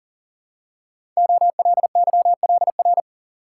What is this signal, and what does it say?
Morse code, heard as a keyed tone, about 42 wpm, 690 Hz. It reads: OLYLR
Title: Code Groups - 5 Letter:Numbers - 40wpm